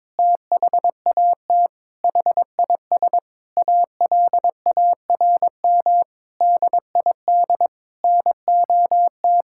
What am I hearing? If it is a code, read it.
Morse code, heard as a keyed tone, about 22 wpm, 700 Hz. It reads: THAT HIS ALARM DID NOT
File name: practice / 22wpm / Sentences with 4-Word Phrases - 22wpm